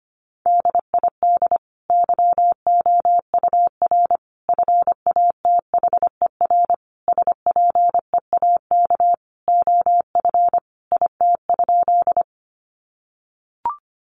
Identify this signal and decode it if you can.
Morse code, heard as a keyed tone, about 25 wpm, 700 Hz. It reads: DID YOUR FATHER SPEAK OF IT?
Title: Sentences from Top 1100 Words 25wpm